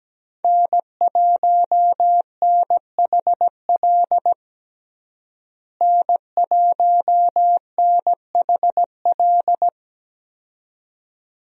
Morse code, heard as a keyed tone, about 17 wpm, 700 Hz. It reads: N1NHL N1NHL
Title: USA Call Signs - 5 Characters - Repeated 2x - 17wpm